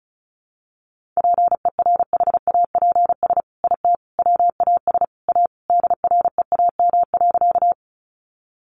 Morse code, heard as a keyed tone, about 34 wpm, 700 Hz. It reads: PERHAPS IT WAS A DREAM.
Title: Sentences from Top 1500 Words 35wpm